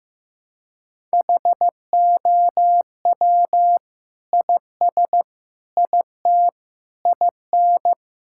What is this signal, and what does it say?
Morse code, heard as a keyed tone, about 15 wpm, 700 Hz. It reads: HOW IS IT IN
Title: Sentences from Top 1000 Words - Review Part 5 - 15wpm